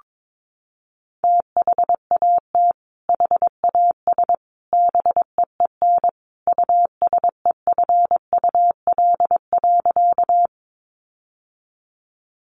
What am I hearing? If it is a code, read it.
Morse code, heard as a keyed tone, about 22 wpm, 700 Hz. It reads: THAT HAS BEEN USEFUL.
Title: Sentences from Top 700 Words 22wpm